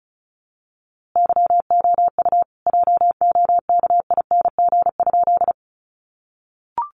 Morse code, heard as a keyed tone, about 34 wpm, 700 Hz. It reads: YOU JOKING?
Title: A Sound Of Thunder 35wpm